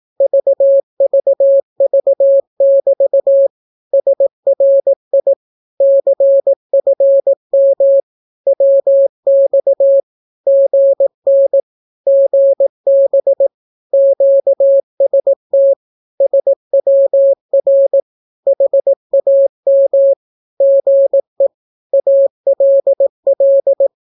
Morse code, heard as a keyed tone, about 18 wpm, 550 Hz. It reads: VVV<BT> SRI CFM WX GN GB QST SWR HAM GE ALL
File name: naukapodstaw / skroty / skroty